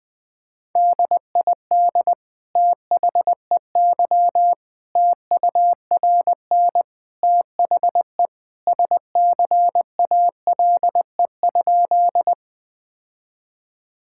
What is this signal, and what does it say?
Morse code, heard as a keyed tone, about 20 wpm, 700 Hz. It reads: DID THEY TURN THE SCALE?
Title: Sentences from Top 1000 Words - Review Part 5 - 20wpm